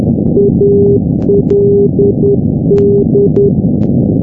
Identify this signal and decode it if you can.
Morse code, heard as a keyed tone, about 10 wpm, 400 Hz, with noise. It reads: ALD
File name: europe / g-910-ald-383-sdr-hu-g-south west uk- -1245z